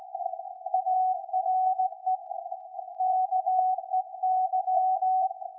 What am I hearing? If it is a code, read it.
Morse code, heard as a keyed tone, about 17 wpm, 750 Hz. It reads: ARE KEY
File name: iz7slz / VLF / SAQ 13-02-2023